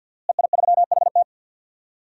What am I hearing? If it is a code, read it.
Morse code, heard as a keyed tone, about 51 wpm, 700 Hz. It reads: EI3HT
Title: DXCC-100 Call Signs - 5 Characters - Repeated 3x - 50wpm